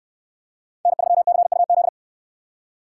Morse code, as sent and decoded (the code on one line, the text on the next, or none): -. ...-- -.-. ..- --..
N3CUZ